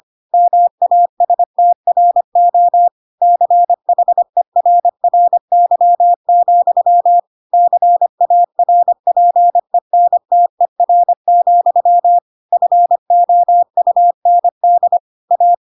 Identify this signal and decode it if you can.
Morse code, heard as a keyed tone, about 25 wpm, 700 Hz. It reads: MASTRO CHERRY, CARPENTER, FOUND A
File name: Pinocchio / Pinocchio ch1 25 25 ik1hge